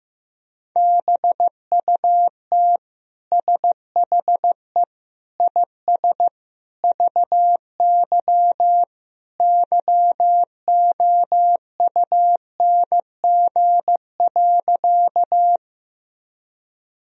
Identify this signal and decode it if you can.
Morse code, heard as a keyed tone, about 16 wpm, 700 Hz. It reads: BUT SHE IS VY YOUNG.
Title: Sentences from Top 400 Words and Abbreviations 15wpm